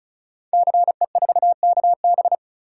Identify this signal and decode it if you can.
Morse code, heard as a keyed tone, about 36 wpm, 700 Hz. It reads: CE4KB
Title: Code Groups - 5 Letter:Numbers - 35wpm